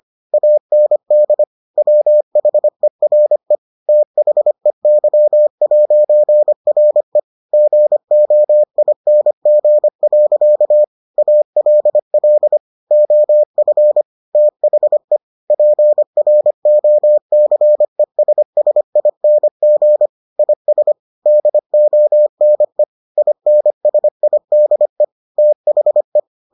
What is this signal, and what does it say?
Morse code, heard as a keyed tone, about 25 wpm, 600 Hz. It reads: AND WHERE THEY'RE GOING. ALL OF THE PROCESSING IS DONE INSIDE THE